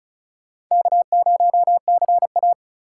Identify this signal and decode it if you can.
Morse code, heard as a keyed tone, about 44 wpm, 700 Hz. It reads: K0CA